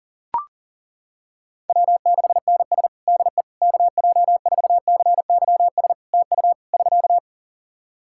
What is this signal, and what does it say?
Morse code, heard as a keyed tone, about 37 wpm, 700 Hz. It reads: W6NS DE KJ4CYS TU <SK>
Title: Practice QSO - Exchange 4 - Ending - 40wpm